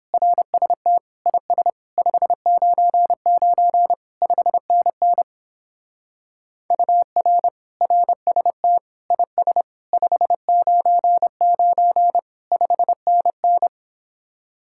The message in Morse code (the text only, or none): RST IS 599 5NN UR RST IS 599 5NN